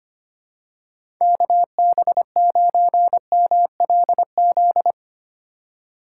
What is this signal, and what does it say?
Morse code, heard as a keyed tone, about 25 wpm, 700 Hz. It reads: KB9MLZ
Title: USA Call Signs - 6 Characters - 4x Character Spacing - 25wpm